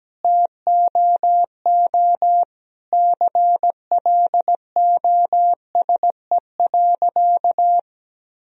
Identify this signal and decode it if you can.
Morse code, heard as a keyed tone, about 17 wpm, 700 Hz. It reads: TOO CLOSE.
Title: Sentences from Top 1000 Words - Review Part 1 - 17wpm